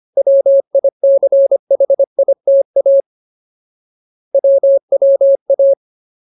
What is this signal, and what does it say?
Morse code, heard as a keyed tone, about 25 wpm, 550 Hz. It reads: WICHITA WWA
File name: Ultimate Word List / 25WPM Ultimate